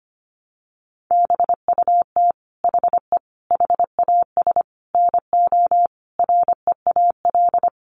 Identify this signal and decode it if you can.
Morse code, heard as a keyed tone, about 25 wpm, 700 Hz. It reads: BUT HE HAS NO REAL
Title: Sentences from Top 1300 Words 25wpm